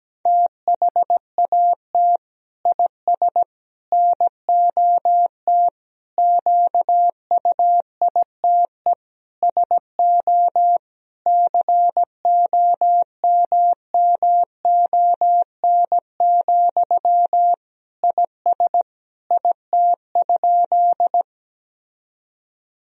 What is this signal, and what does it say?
Morse code, heard as a keyed tone, about 17 wpm, 700 Hz. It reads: THAT IS NOT QUITE SO COMMON, IS IT?
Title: Sentences from Top 1000 Words - Review Part 3 - 17wpm